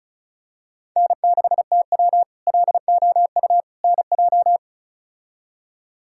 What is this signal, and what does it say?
Morse code, heard as a keyed tone, about 36 wpm, 700 Hz. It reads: N6TW LOU NJ
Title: K1USN SST QSO Practice - 35wpm